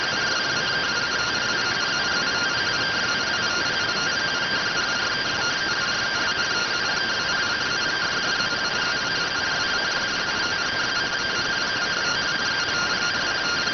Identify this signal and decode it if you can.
Two-tone teleprinter signal, about 170 Hz shift.